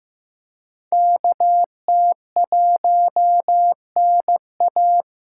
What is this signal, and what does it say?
Morse code, heard as a keyed tone, about 18 wpm, 700 Hz. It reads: KT1NA
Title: Speed Racing - USA 2x2 Call Signs 15wpm